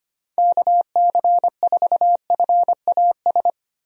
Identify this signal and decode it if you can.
Morse code, heard as a keyed tone, about 25 wpm, 700 Hz. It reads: KC4FAS
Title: USA Call Signs - 6 Characters - 25wpm